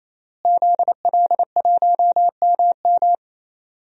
Morse code, heard as a keyed tone, about 28 wpm, 700 Hz. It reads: ZL1MM
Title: DXCC-100 Call Signs - 5 Characters - Repeated 3x - 28wpm